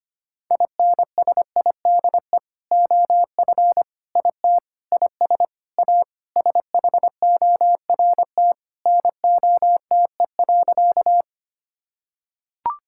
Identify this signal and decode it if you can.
Morse code, heard as a keyed tone, about 25 wpm, 700 Hz. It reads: INSIDE OF IT IS A SHORT NOTE.
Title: Sentences from Top 1000 Words - Review Part 6 - 25wpm